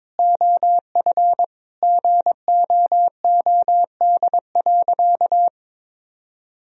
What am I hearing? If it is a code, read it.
Morse code, heard as a keyed tone, about 22 wpm, 700 Hz. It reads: OF GOOD.
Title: Sentences from Top 1000 Words - Review Part 3 - 22wpm